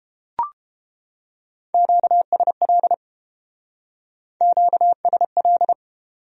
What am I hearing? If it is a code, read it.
Morse code, heard as a keyed tone, about 33 wpm, 700 Hz. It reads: QSL QSL